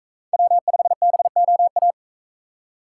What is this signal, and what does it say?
Morse code, heard as a keyed tone, about 44 wpm, 700 Hz. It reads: W5BOA